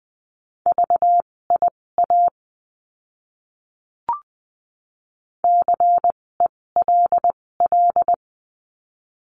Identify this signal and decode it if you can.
Morse code, heard as a keyed tone, about 20 wpm, spaced as for 16 wpm, 700 Hz. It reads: VIA CELL
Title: 256 Lesson 42a - Words 12-20wpm